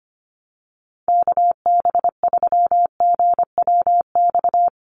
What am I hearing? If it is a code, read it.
Morse code, heard as a keyed tone, about 25 wpm, 700 Hz. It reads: KB3GWX
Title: USA Call Signs 25wpm